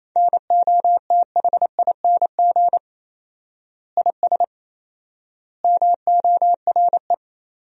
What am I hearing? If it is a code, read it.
Morse code, heard as a keyed tone, about 28 wpm, 700 Hz. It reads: NOTHING IS MORE
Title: 4x Word-Spacing - Sentences from Top 500 Words - 28wpm